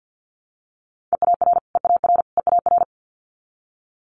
Morse code, heard as a keyed tone, about 50 wpm, 700 Hz. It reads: EAR EAR EAR
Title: Multi-letter Phonemes Repeated 3x 50wpm